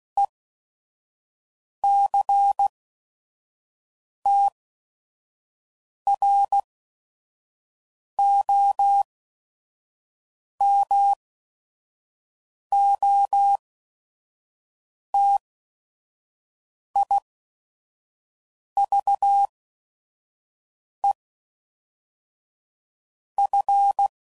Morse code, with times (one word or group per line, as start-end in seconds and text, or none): 0.2-0.3 E
1.8-2.7 C
4.3-4.5 T
6.1-6.6 R
8.2-9.0 O
10.6-11.1 M
12.7-13.6 O
15.1-15.4 T
17.0-17.2 I
18.8-19.5 V
21.0-21.1 E
23.4-24.1 F